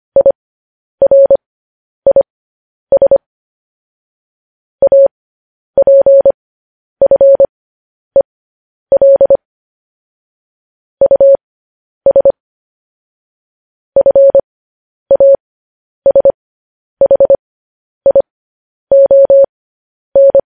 Morse code, heard as a keyed tone, about 25 wpm, spaced as for 10 wpm, 550 Hz. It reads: IRIS APFEL US FASHION